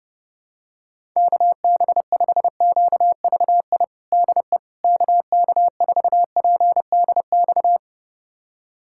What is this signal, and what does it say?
Morse code, heard as a keyed tone, about 30 wpm, 700 Hz. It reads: KB5QVI DE KK4PDX
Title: USA Callsign DE Callsign - 6 Characters - 30wpm